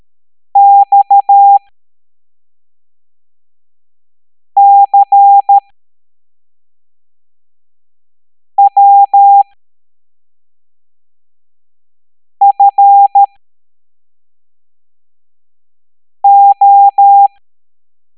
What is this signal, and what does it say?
Morse code, heard as a keyed tone, about 13 wpm, spaced as for 3 wpm, 800 Hz. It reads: XCWFO